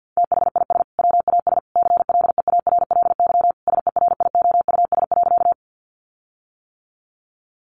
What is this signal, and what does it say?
Morse code, heard as a keyed tone, about 50 wpm, 700 Hz. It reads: THIS WAS CLEARLY SERIOUS.
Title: Sentences from Top 1300 Words 50wpm